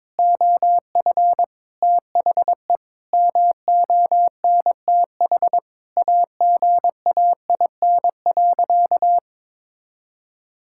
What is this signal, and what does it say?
Morse code, heard as a keyed tone, about 22 wpm, 700 Hz. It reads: OF THE MONTH AGAIN.
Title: Sentences from Top 1000 Words - Review Part 5 - 22wpm